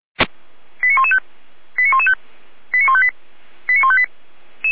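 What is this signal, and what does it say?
Five-tone selective call (ZVEI), heard as quick runs of five tones, tones of about 70 ms. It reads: 89115 89115 89158 89158
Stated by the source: ZVEI five-tone call received by SDR, from the Feuerwehr Orlamünde (fire brigade) server.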